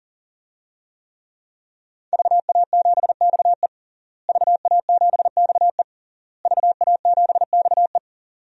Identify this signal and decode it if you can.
Morse code, heard as a keyed tone, about 40 wpm, 700 Hz. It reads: VA7XE VA7XE VA7XE
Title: DXCC-100 Call Signs - 5 Characters - Repeated 3x - 40wpm